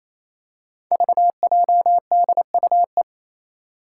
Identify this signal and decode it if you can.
Morse code, heard as a keyed tone, about 28 wpm, 700 Hz. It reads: VJDUE
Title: Five Challenging Letter-Number Combinations 28wpm